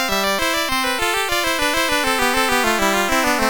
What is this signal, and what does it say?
There is no code.